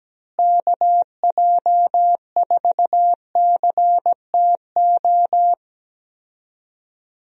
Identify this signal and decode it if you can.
Morse code, heard as a keyed tone, about 17 wpm, 700 Hz. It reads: KJ4CTO
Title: USA Call Signs - 6 Characters - Repeated 2x - 17wpm